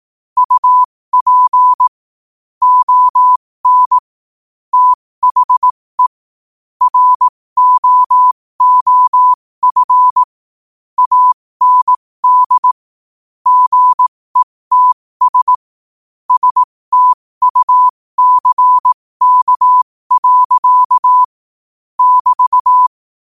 Morse code, heard as a keyed tone, about 18 wpm, spaced as for 15 wpm, 1000 Hz. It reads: UP ON THE ROOF AND GETS STUCK. <BT>
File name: qotdfiles / 15wpm